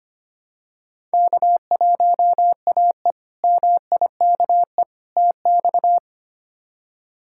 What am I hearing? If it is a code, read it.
Morse code, heard as a keyed tone, about 25 wpm, 700 Hz. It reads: K1AE MIKE TX
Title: K1USN SST QSO Practice - 25wpm